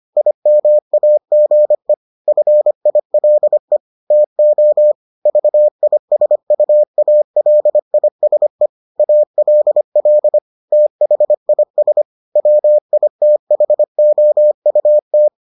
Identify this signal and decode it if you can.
Morse code, heard as a keyed tone, about 25 wpm, 600 Hz. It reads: IMAGE FILE TO VISUALISE ALL THIS WITHOUT